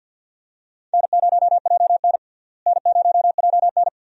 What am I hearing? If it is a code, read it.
Morse code, heard as a keyed tone, about 52 wpm, 700 Hz. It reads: N0JN N0JN
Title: USA Call Signs - 4 Characters - Repeated 2x - 50wpm